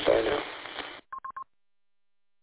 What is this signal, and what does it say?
There is no code.